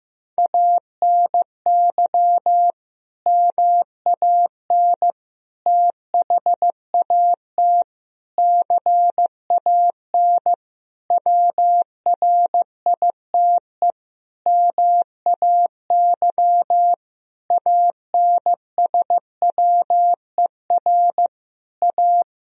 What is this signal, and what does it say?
Morse code, heard as a keyed tone, about 15 wpm, 700 Hz. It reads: ANY MAN THAT CAN WRITE MAY ANSWER A
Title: Sentences from Top 1000 Words - Review Part 1 - 15wpm